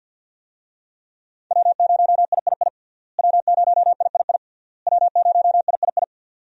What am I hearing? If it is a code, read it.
Morse code, heard as a keyed tone, about 52 wpm, 700 Hz. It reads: W0III W0III W0III
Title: USA Call Signs - 5 Characters - Repeated 3x - 50wpm